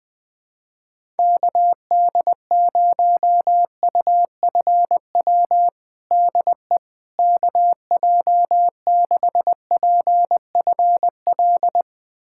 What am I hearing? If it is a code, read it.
Morse code, heard as a keyed tone, about 20 wpm, 700 Hz. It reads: KD0UFW DE KJ6PFL
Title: USA Callsign DE Callsign - 6 Characters - 20wpm